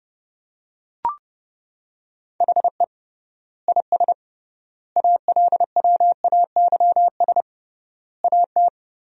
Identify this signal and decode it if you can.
Morse code, heard as a keyed tone, about 32 wpm, 700 Hz. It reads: HE IS ALWAYS AT
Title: 3x Word-Spacing - Sentences from Top 300 Words - 30wpm